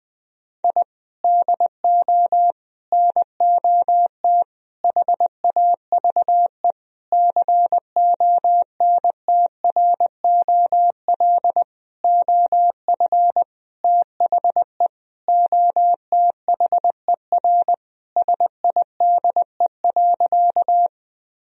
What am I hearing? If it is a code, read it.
Morse code, heard as a keyed tone, about 20 wpm, 700 Hz. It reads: I DO NOT HAVE CONTROL OF THE OTHER SIDE.
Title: Sentences from Top 1000 Words - Review Part 3 - 20wpm